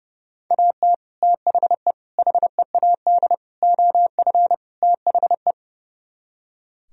Morse code, heard as a keyed tone, about 38 wpm, 700 Hz. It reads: AT THE HEAD OF THE